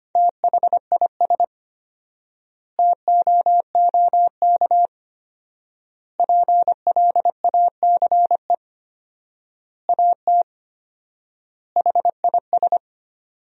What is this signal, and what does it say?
Morse code, heard as a keyed tone, about 25 wpm, 700 Hz. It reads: THIS TOOK PLACE AT HIS
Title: 4x Word-Spacing - Sentences from Top 300 Words - 25wpm